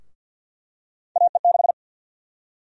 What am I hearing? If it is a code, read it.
Morse code, heard as a keyed tone, about 49 wpm, 700 Hz. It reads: AE6
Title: USA Call Sign Prefixes - 50wpm